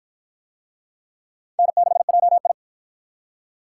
Morse code, heard as a keyed tone, about 53 wpm, 700 Hz. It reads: N6JI